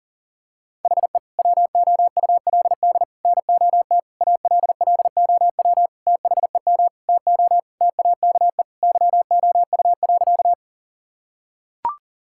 Morse code, heard as a keyed tone, about 40 wpm, 700 Hz. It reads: HE WOULD NOT ALLOW THEM TO TAKE YOU.
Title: Sentences from Top 1000 Words - Review Part 5 - 40wpm